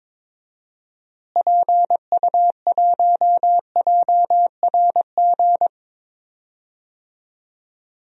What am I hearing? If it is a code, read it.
Morse code, heard as a keyed tone, about 21 wpm, 700 Hz. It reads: PU1JRG
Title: DXCC-100 Call Signs - 6 Characters - Repeated 2x - 22wpm